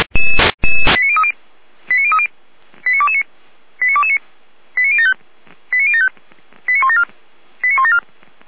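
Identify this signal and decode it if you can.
Five-tone selective call (ZVEI), heard as quick runs of five tones, tones of about 70 ms. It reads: 89030 89030 89229 89229 89075 89075 89174 89174
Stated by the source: ZVEI five-tone selective call received by SDR.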